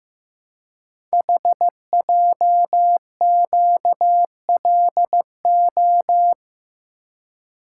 Morse code, heard as a keyed tone, about 15 wpm, 700 Hz. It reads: HJQLO
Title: Code Groups - 5 Letters - 15wpm